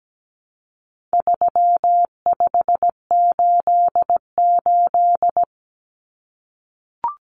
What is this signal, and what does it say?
Morse code, heard as a keyed tone, about 17 wpm, 700 Hz. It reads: 3588
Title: Common Cut Numbers - 4-Digit 17wpm